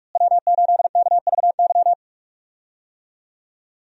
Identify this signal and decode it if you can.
Morse code, heard as a keyed tone, about 42 wpm, 700 Hz. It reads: W8KVY
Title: USA Call Signs - 5 Characters - Repeated 2x - 45wpm